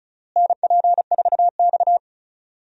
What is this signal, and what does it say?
Morse code, heard as a keyed tone, about 34 wpm, 700 Hz. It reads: NP4X
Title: USA Call Signs - 4 Characters - Repeated 3x - 35wpm